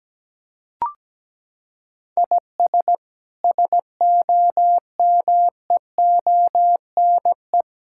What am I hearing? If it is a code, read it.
Morse code, heard as a keyed tone, about 17 wpm, 700 Hz. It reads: IS SOMEONE
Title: Sentences from Top 1000 Words - Review Part 5 - 17wpm